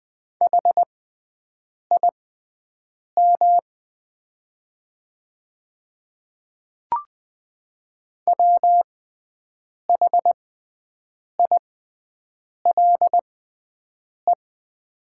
Morse code, heard as a keyed tone, about 20 wpm, spaced as for 7 wpm, 700 Hz. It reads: HIM WHILE
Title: 6x Character Spacing - Top 300 Words - 20wpm